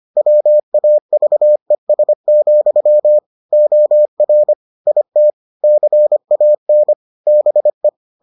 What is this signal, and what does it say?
Morse code, heard as a keyed tone, about 25 wpm, 600 Hz. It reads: WAVES, OR IT CAN BE